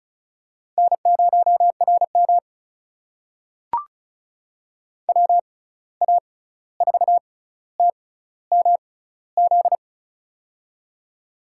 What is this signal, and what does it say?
Morse code, heard as a keyed tone, about 36 wpm, spaced as for 16 wpm, 700 Hz. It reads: N0RM WA4TMZ
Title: USA Call Signs - 6x Character Spacing - 35wpm